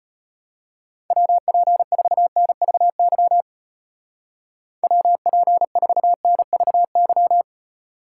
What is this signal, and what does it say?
Morse code, heard as a keyed tone, about 31 wpm, 700 Hz. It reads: WP4NVY WP4NVY